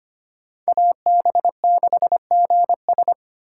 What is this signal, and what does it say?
Morse code, heard as a keyed tone, about 25 wpm, 700 Hz. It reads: AB6GS
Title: USA Call Signs - 5 Characters - 4x Character Spacing - 25wpm